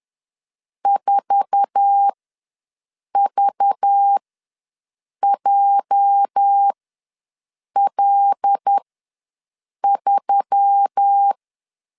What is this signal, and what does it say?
Morse code, heard as a keyed tone, about 10 wpm, spaced as for 6 wpm, 800 Hz. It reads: 4VJL3
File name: LEZIONI / 8 YJ45 / 8 YJ4 ws 10wpm 4020